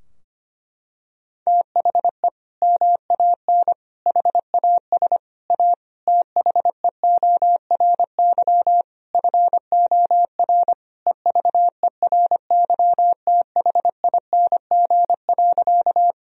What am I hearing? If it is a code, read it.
Morse code, heard as a keyed tone, about 25 wpm, 700 Hz. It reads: THE MAN HAS A THEORY FOR EVERYTHING.
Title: Sentences from Top 1000 Words - Review Part 3 - 25wpm